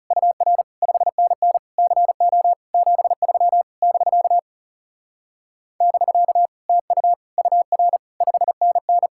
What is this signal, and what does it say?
Morse code, heard as a keyed tone, about 40 wpm, 700 Hz. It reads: UR 5NN CO 73 <BK> <BK> TU UR 5NN